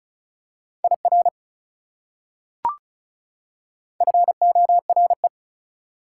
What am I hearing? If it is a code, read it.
Morse code, heard as a keyed tone, about 34 wpm, 700 Hz. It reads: IR FORE
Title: Prefixes and Suffixes 35wpm